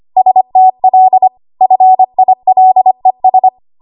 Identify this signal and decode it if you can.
Morse code, heard as a keyed tone, about 25 wpm, 750 Hz. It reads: STL FILES
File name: Archive / 25 WPM / 25WPM